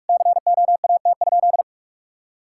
Morse code, heard as a keyed tone, about 41 wpm, 700 Hz. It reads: COAT?
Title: Sentences from Top 1600 Words 45wpm